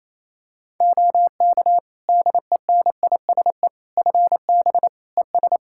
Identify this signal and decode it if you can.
Morse code, heard as a keyed tone, about 28 wpm, 700 Hz. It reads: OK DENISE FB ES
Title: Practice QSO - Exchange 2 - Info - 28wpm